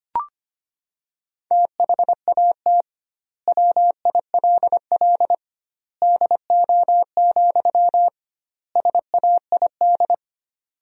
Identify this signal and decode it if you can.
Morse code, heard as a keyed tone, about 25 wpm, 700 Hz. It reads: THAT WILL DO, SAID
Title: 2x Word-Spacing - Sentences from Top 100 Words - 25wpm